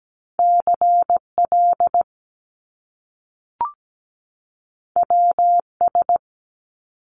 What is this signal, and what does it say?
Morse code, heard as a keyed tone, about 17 wpm, 700 Hz. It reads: CL WS
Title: Top Two Letter Combinations in Words 17wpm